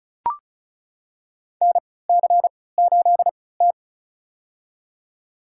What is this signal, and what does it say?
Morse code, heard as a keyed tone, about 34 wpm, spaced as for 20 wpm, 700 Hz. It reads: NC8T
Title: USA Call Signs - 4 Characters - 3x Character Spacing - 35wpm